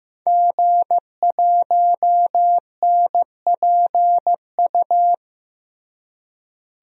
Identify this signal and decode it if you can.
Morse code, heard as a keyed tone, about 15 wpm, 700 Hz. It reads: G1NPU
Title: DXCC-100 Call Signs - 5 Characters - Repeated 2x - 15wpm